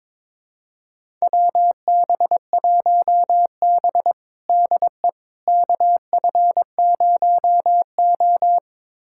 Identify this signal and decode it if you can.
Morse code, heard as a keyed tone, about 22 wpm, 700 Hz. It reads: WB1B DE KF0O
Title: USA Callsign DE Callsign - 4 Characters - 22wpm